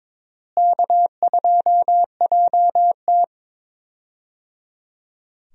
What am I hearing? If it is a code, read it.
Morse code, heard as a keyed tone, about 22 wpm, 700 Hz. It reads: K2JT
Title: USA Call Signs - 4 Characters - Repeated 3x - 22wpm